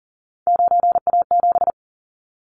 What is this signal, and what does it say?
Morse code, heard as a keyed tone, about 40 wpm, 700 Hz. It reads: 9A7
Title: Callsign Prefixes for Top 25 DXCC - 40wpm